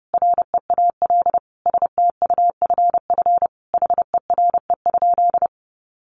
Morse code, heard as a keyed tone, about 30 wpm, 700 Hz. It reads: REAL STUFF HERE?
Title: Sentences from Top 1400 Words 30wpm